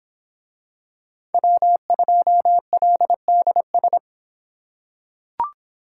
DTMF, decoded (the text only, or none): none